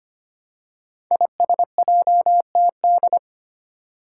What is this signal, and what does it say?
Morse code, heard as a keyed tone, about 25 wpm, 700 Hz. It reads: ISJTD
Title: Five Challenging Letter-Number Combinations 25wpm